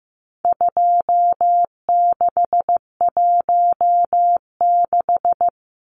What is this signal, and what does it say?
Morse code, heard as a keyed tone, about 15 wpm, 700 Hz. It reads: 2616
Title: Numbers - 4-Digit 15wpm